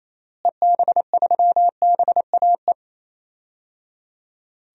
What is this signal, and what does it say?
Morse code, heard as a keyed tone, about 28 wpm, 700 Hz. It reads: EB3BAE
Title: DXCC-100 Call Signs - 6 Characters - Repeated 3x - 28wpm